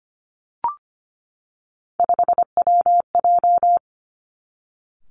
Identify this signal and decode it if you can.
Morse code, heard as a keyed tone, about 25 wpm, 700 Hz. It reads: 5WJ
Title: Three Challenging Letter-Number Combinations 25wpm